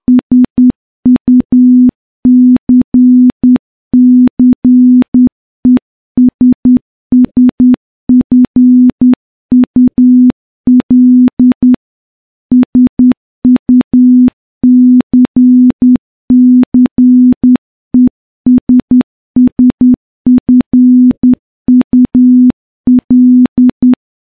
Morse code, heard as a keyed tone, about 10 wpm, 250 Hz. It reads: SUCCESSFUL SUCCESSFUL
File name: morse-training-10WPM